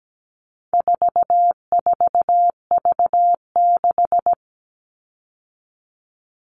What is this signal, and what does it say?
Morse code, heard as a keyed tone, about 17 wpm, 700 Hz. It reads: 44V6
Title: All Cut Numbers - 4-Digit 17wpm